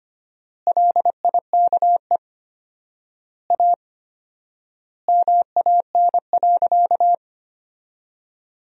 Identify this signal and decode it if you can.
Morse code, heard as a keyed tone, about 25 wpm, 700 Hz. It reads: LIKE A MAN.
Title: 4x Word-Spacing - Sentences from Top 300 Words - 25wpm